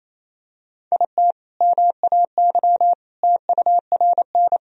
Morse code, heard as a keyed tone, about 28 wpm, 700 Hz. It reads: IT MAY TURN
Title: Sentences from Top 400 Words and Abbreviations 28wpm